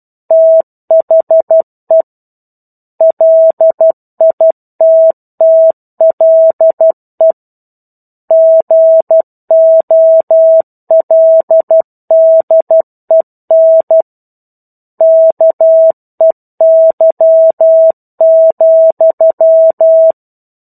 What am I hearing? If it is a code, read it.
Morse code, heard as a keyed tone, about 12 wpm, 650 Hz. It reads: THE LITTLE GOLDEN KEY,
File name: 01-Chapter-01 12 WPM